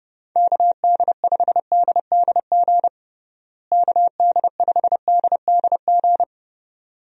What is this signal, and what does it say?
Morse code, heard as a keyed tone, about 32 wpm, 700 Hz. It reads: KD5DDG KD5DDG
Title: USA Call Signs - 6 Characters - Repeated 2x - 30wpm